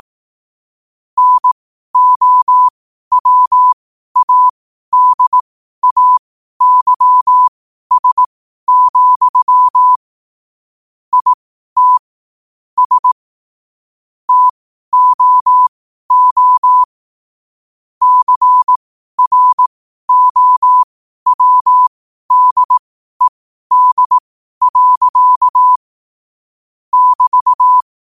Morse code, heard as a keyed tone, about 18 wpm, spaced as for 12 wpm, 1000 Hz. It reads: NOWADAYS, IT S TOO CROWDED. <BT>